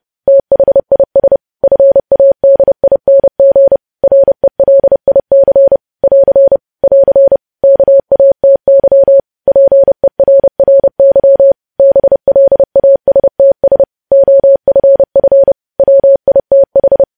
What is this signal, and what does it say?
Morse code, heard as a keyed tone, about 30 wpm, 550 Hz. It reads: THIS FADING RELIC <AR> <AR> KATY PERRY BLASTS OFF WITH
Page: news headlines in morse code at 30 wpm